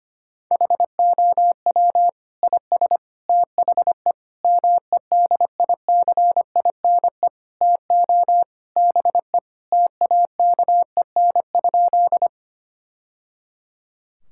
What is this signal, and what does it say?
Morse code, heard as a keyed tone, about 25 wpm, 700 Hz. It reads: HOW IS THE MEDICINE TO BE TAKEN?
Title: Sentences from Top 1000 Words - Review Part 6 - 25wpm